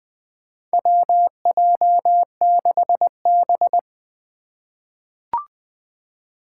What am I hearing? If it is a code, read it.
Morse code, heard as a keyed tone, about 20 wpm, 700 Hz. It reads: WJ6B
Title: Four Challenging Letter-Number Combinations 20wpm